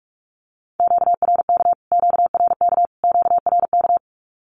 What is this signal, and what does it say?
Morse code, heard as a keyed tone, about 44 wpm, 700 Hz. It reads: QRK QRK QRK
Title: Q-Codes Repeated 3x 45wpm